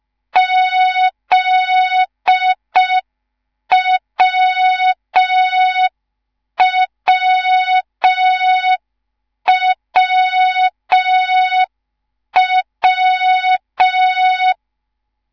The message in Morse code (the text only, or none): ZWWWW